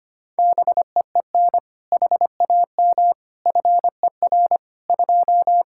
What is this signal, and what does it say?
Morse code, heard as a keyed tone, about 25 wpm, 700 Hz. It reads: BEEN HAM FER 2
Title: K1USN SST QSO Practice 25wpm